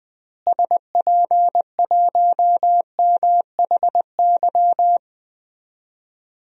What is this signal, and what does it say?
Morse code, heard as a keyed tone, about 21 wpm, 700 Hz. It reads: SP1MHY
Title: DXCC-100 Call Signs - 6 Characters - Repeated 3x - 20wpm